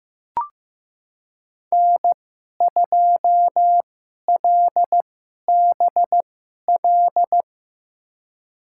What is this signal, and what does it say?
Morse code, heard as a keyed tone, about 15 wpm, spaced as for 11 wpm, 700 Hz. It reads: N2LBL
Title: USA Call Signs - 2x Character Spacing - 15wpm